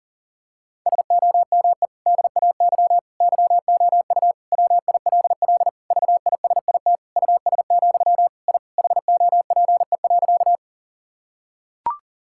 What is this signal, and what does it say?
Morse code, heard as a keyed tone, about 40 wpm, 700 Hz. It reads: SOME DAY YOU WILL VISIT US, I HOPE.
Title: Sentences from Top 1000 Words - Review Part 6 - 40wpm